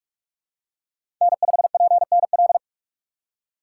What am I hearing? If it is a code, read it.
Morse code, heard as a keyed tone, about 44 wpm, 690 Hz. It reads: N5PNL
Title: USA Call Signs - 5 Characters - 4x Character Spacing - 45wpm